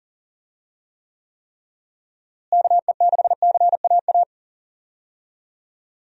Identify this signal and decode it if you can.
Morse code, heard as a keyed tone, about 37 wpm, 700 Hz. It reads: KE6CAA
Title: USA Call Signs - 6 Characters - Repeated 3x - 40wpm